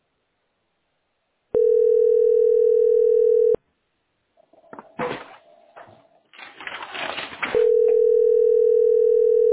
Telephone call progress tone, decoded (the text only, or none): ringback tone